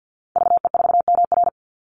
Morse code, heard as a keyed tone, about 50 wpm, 700 Hz. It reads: VE4MR
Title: Call Signs Repeated 2x - 50wpm